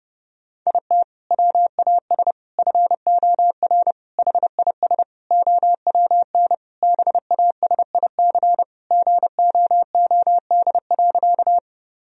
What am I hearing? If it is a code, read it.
Morse code, heard as a keyed tone, about 31 wpm, 700 Hz. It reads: IT WAS FOR HIS OWN BASIC GOOD.
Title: Sentences from Top 1000 Words - Review Part 3 - 30wpm